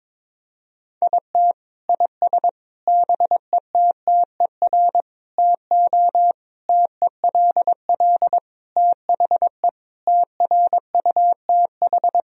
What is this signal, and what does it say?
Morse code, heard as a keyed tone, about 22 wpm, 700 Hz. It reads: IT IS BETTER TO TELL THE TRUTH
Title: Sentences from Top 1000 Words - Review Part 4 - 22wpm